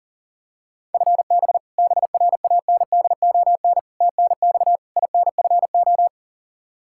Morse code, heard as a keyed tone, about 37 wpm, 700 Hz. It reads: FB BRANDON TNX INFO
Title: Practice QSO - Exchange 3 - More Info - 40wpm